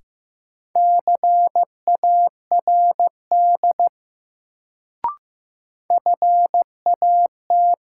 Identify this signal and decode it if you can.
Morse code, heard as a keyed tone, about 15 wpm, 700 Hz. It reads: CARD FAT
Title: Sentences from Top 600 Words - New Word Drill - 15wpm